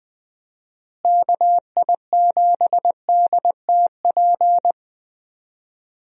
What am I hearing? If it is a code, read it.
Morse code, heard as a keyed tone, about 20 wpm, 700 Hz. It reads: KI7DTP
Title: USA Call Signs - 6 Characters - Repeated 3x - 20wpm